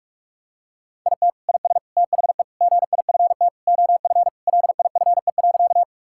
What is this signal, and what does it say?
Morse code, heard as a keyed tone, about 41 wpm, 700 Hz. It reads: IT IS THE GIFT OF LIFE.
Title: Sentences from Top 1000 Words - Review Part 6 - 45wpm